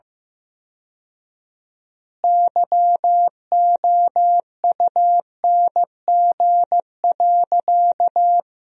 Morse code, heard as a keyed tone, about 15 wpm, 700 Hz. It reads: YOUNG.
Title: 4x Word-Spacing - Sentences from Top 400 Words - 15wpm